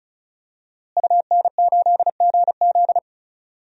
Morse code, heard as a keyed tone, about 36 wpm, 700 Hz. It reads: UN8GZ
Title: DXCC-100 Call Signs - 5 Characters - Repeated 3x - 35wpm